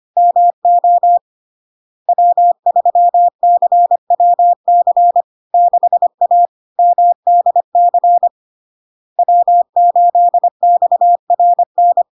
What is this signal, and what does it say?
Morse code, heard as a keyed tone, about 25 wpm, 700 Hz. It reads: MO W3CWC 6A MDC W8XRN
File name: LARK-FD-2022-Log-in-CW-25wpm-700-Hz